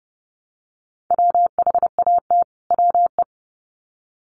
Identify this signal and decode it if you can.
Morse code, heard as a keyed tone, about 30 wpm, 700 Hz. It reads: WHAT WE
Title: Sets of 2 Words 30wpm